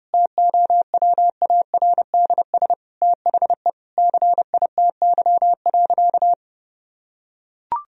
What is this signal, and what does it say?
Morse code, heard as a keyed tone, about 30 wpm, 700 Hz. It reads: TOWARDS THE CITY.
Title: Sentences from Top 1000 Words - Review Part 2 - 30wpm